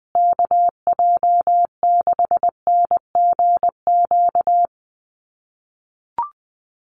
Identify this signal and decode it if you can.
Morse code, heard as a keyed tone, about 20 wpm, 700 Hz. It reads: KJ6NGQ
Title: USA Call Signs Repeated 2x 20wpm